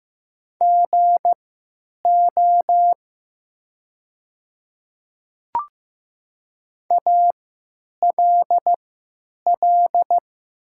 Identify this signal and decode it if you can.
Morse code, heard as a keyed tone, about 15 wpm, spaced as for 9 wpm, 700 Hz. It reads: GO ALL
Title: 3x Character Spacing - Top 100 Words - 15wpm